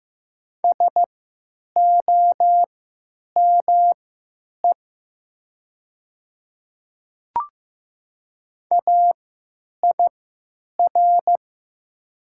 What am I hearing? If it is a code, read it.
Morse code, heard as a keyed tone, about 15 wpm, spaced as for 9 wpm, 700 Hz. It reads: SOME AIR
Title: 3x Character Spacing - Top 200 Words - 15wpm